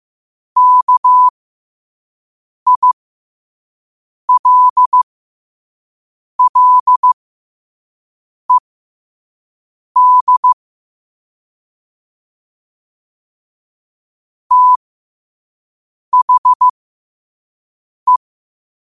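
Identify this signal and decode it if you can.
Morse code, heard as a keyed tone, about 15 wpm, spaced as for 5 wpm, 1000 Hz. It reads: KILLED THE
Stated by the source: Morse code sent at 5 wpm.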